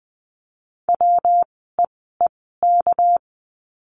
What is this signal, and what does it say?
Morse code, heard as a keyed tone, about 20 wpm, spaced as for 14 wpm, 700 Hz. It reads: WEEK
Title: 199 Lesson 32f - Letter k - Words 20wpm